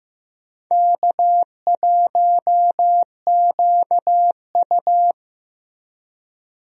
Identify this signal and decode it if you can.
Morse code, heard as a keyed tone, about 15 wpm, 700 Hz. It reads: K1QU
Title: USA Call Signs - 4 Characters - Repeated 3x - 15wpm